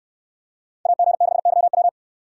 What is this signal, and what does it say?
Morse code, heard as a keyed tone, about 67 wpm, 700 Hz. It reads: AG6YW